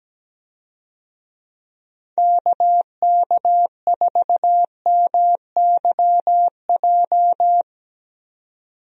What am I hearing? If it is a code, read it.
Morse code, heard as a keyed tone, about 17 wpm, 700 Hz. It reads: KK4MYJ
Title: USA Call Signs - 6 Characters - Repeated 2x - 17wpm